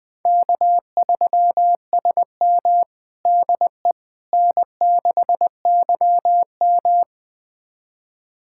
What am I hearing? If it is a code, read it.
Morse code, heard as a keyed tone, about 20 wpm, 700 Hz. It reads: K3SM DE N6YM
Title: USA Callsign DE Callsign - 4 Characters - 20wpm